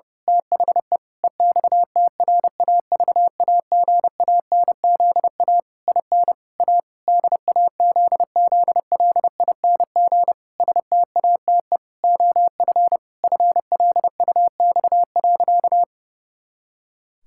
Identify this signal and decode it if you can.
Morse code, heard as a keyed tone, about 30 wpm, 700 Hz. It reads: THE EXTRAVAGANZA IN A DAZZLING STATE OF FLUX.
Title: Pangram Challenge v2 - 30wpm